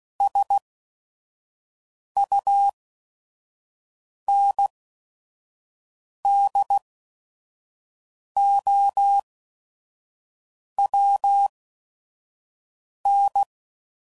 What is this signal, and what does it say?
Morse code, heard as a keyed tone, about 16 wpm, spaced as for 5 wpm, 800 Hz. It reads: SUNDOWN
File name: CW Code Groups @5WPM / fromSM's / Cities